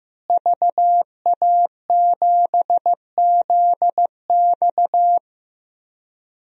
Morse code, heard as a keyed tone, about 15 wpm, 700 Hz. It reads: VA7ZX
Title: DXCC-100 Call Signs - 5 Characters - Repeated 2x - 15wpm